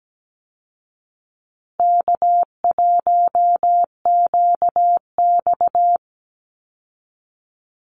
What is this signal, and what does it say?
Morse code, heard as a keyed tone, about 17 wpm, 700 Hz. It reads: K1QX
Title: Easy Call Signs Repeated 3x - 17wpm